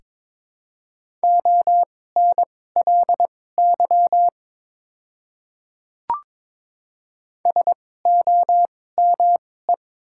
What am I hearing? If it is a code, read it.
Morse code, heard as a keyed tone, about 22 wpm, spaced as for 16 wpm, 700 Hz. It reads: ONLY SOME
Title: 2x Character Spacing - Top 500 Words - 22wpm